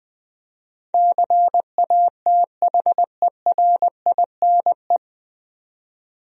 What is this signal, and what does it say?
Morse code, heard as a keyed tone, about 20 wpm, 700 Hz. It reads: CATHERINE
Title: USA Common Names - 20wpm